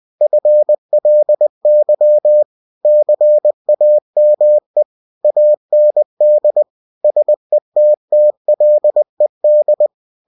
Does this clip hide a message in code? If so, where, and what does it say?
Yes: Morse code from 0.2 to 9.9 s, heard as a keyed tone, about 20 wpm, 600 Hz. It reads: FLY CAME AND SETTLED